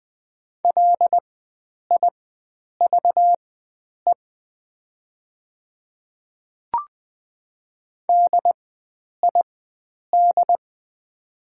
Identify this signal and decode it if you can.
Morse code, heard as a keyed tone, about 20 wpm, spaced as for 9 wpm, 700 Hz. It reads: LIVE DID
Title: 4x Character Spacing - Top 500 Words - 20wpm